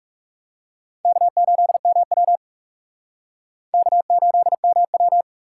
Morse code, heard as a keyed tone, about 38 wpm, 700 Hz. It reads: K8MW K8MW